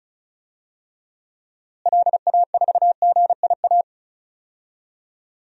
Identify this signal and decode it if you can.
Morse code, heard as a keyed tone, about 34 wpm, 700 Hz. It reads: LA4GIA
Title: DXCC-100 Call Signs - 6 Characters - Repeated 2x - 35wpm